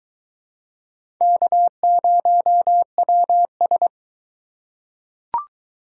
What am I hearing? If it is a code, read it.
Morse code, heard as a keyed tone, about 21 wpm, 700 Hz. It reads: K0WS